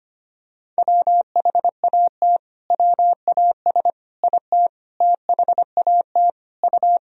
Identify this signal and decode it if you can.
Morse code, heard as a keyed tone, about 25 wpm, 700 Hz. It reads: WHAT WAS IT THAT U
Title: Sentences from Top 300 Words and Abbreviations 25wpm